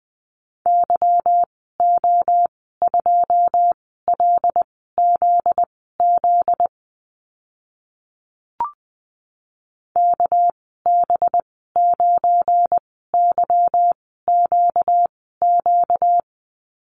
Morse code, heard as a keyed tone, about 20 wpm, spaced as for 14 wpm, 700 Hz. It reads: YO2LZZ KB9YQQ
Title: 245 Lesson 39g - Letter z - Call Signs 20wpm